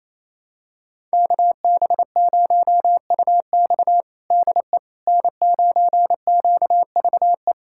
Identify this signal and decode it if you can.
Morse code, heard as a keyed tone, about 28 wpm, 700 Hz. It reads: KB0UX DE N9QVE
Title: USA Callsign DE Callsign 28wpm